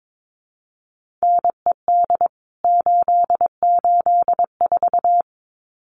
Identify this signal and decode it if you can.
Morse code, heard as a keyed tone, about 22 wpm, 700 Hz. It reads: NED 884
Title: CWT Exchange 22wpm